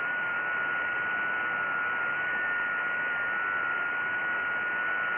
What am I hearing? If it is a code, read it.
FT8.
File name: IC-7100